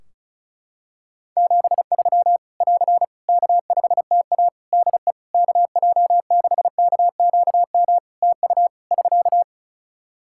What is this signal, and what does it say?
Morse code, heard as a keyed tone, about 36 wpm, 700 Hz. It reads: 73 <AR> K5TA DE KJ6KQM TU <SK>
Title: Practice QSO - Exchange 4 - Ending - 35wpm